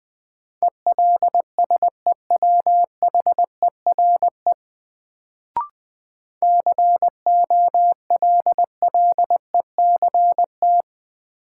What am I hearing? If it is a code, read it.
Morse code, heard as a keyed tone, about 20 wpm, 700 Hz. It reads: ELSEWHERE COLLECT
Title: Sentences from Top 1600 Words - New Word Drill Alt - 20wpm